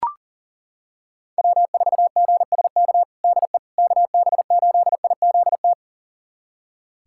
Morse code, heard as a keyed tone, about 40 wpm, 700 Hz. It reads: W4GSK DE KB8IZT
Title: USA Callsign DE Callsign 40wpm